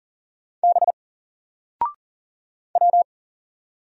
Morse code, heard as a keyed tone, about 40 wpm, 700 Hz. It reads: B W
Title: Single Letter-Number 40wpm